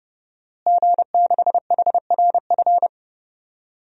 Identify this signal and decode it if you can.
Morse code, heard as a keyed tone, about 30 wpm, 700 Hz. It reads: G6HRF
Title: DXCC-100 Call Signs - 5 Characters - Repeated 2x - 30wpm